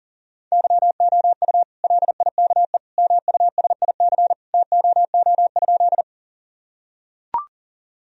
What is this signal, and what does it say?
Morse code, heard as a keyed tone, about 40 wpm, 700 Hz. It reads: YOU LIKE MUSIC TOO?
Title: Sentences from Top 1000 Words - Review Part 1 - 40wpm